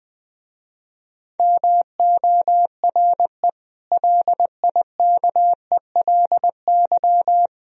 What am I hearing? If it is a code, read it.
Morse code, heard as a keyed tone, about 21 wpm, 700 Hz. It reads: MORE LIKELY
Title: Speed-Racing - Sets of 4 Words 20wpm